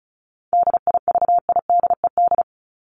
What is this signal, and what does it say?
Morse code, heard as a keyed tone, about 34 wpm, 700 Hz. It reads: DIVIDED
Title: ED Suffix - Top 100 Words with Base Words 35wpm